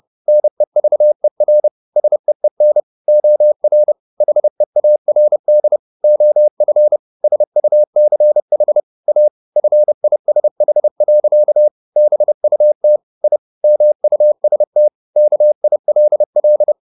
Morse code, heard as a keyed tone, about 30 wpm, 600 Hz. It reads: NEVER SEEN OR HEARD OF SUCH A FISH. BUT I MUST KILL